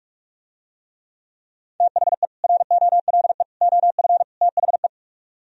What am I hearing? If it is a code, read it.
Morse code, heard as a keyed tone, about 44 wpm, 700 Hz. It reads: THE ROLE OF THE